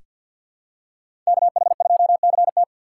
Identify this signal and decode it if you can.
Morse code, heard as a keyed tone, about 47 wpm, 700 Hz. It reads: KH2CT
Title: USA Call Signs - 5 Characters - 4x Character Spacing - 50wpm